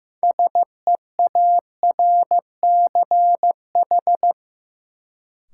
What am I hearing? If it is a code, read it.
Morse code, heard as a keyed tone, about 15 wpm, 700 Hz. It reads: SEARCH
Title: Sentences from Top 800 Words - New Word Drill - 15wpm